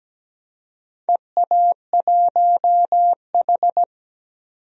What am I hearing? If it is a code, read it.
Morse code, heard as a keyed tone, about 17 wpm, 700 Hz. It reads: EA1H
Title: DXCC-100 Call Signs - 4 Characters - 17wpm